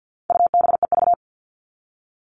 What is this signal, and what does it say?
Morse code, heard as a keyed tone, about 50 wpm, 700 Hz. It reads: UBEV
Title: All Cut Numbers - 4-Digit 50wpm